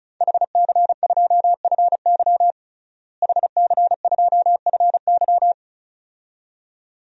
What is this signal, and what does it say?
Morse code, heard as a keyed tone, about 36 wpm, 700 Hz. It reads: HC2FY HC2FY
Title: DXCC-100 Call Signs - 5 Characters - Repeated 2x - 35wpm